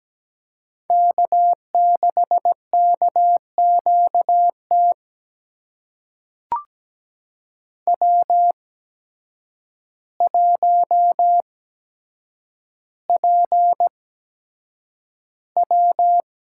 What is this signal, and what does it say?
Morse code, heard as a keyed tone, about 17 wpm, spaced as for 8 wpm, 700 Hz. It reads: K6KQT W1PW
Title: USA Call Signs - 5 Characters - 8x Character Spacing - 17wpm